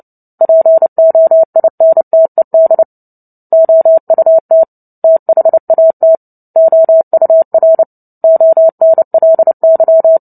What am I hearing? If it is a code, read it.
Morse code, heard as a keyed tone, about 30 wpm, 650 Hz. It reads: POINTED OUT THAT OUR ONLY
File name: Book Two / 03-Chapter 3 - 29 WPM